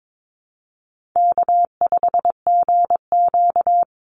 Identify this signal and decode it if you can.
Morse code, heard as a keyed tone, about 22 wpm, 700 Hz. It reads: K5GQ
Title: Call Signs Repeated 3x - 22wpm